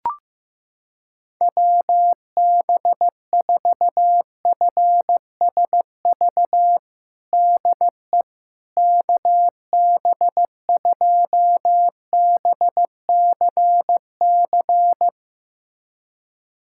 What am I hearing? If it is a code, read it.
Morse code, heard as a keyed tone, about 15 wpm, 700 Hz. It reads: WB4FSV DE KB2BCC
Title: USA Callsign DE Callsign - 6 Characters - 15wpm